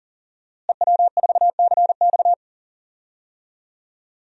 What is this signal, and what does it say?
Morse code, heard as a keyed tone, about 40 wpm, 700 Hz. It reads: EW4CX
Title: DXCC-100 Call Signs - 5 Characters - Repeated 2x - 40wpm